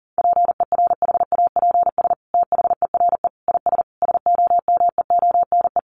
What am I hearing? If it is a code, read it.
Morse code, heard as a keyed tone, about 40 wpm, 700 Hz. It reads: PERHAPS THERE IS SOMEONE
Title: Sentences from Top 600 Words 40wpm